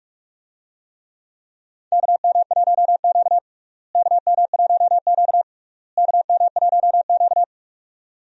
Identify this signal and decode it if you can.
Morse code, heard as a keyed tone, about 46 wpm, 700 Hz. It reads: KM1Q KM1Q KM1Q
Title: USA Call Signs - 4 Characters - Repeated 3x - 45wpm